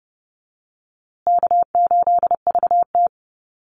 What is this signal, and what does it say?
Morse code, heard as a keyed tone, about 30 wpm, 700 Hz. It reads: K8VT
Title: USA Call Signs Repeated 2x 30wpm